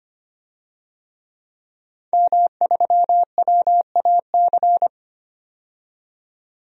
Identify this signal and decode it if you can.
Morse code, heard as a keyed tone, about 25 wpm, 700 Hz. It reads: M3WAC
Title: DXCC-100 Call Signs - 5 Characters - 25wpm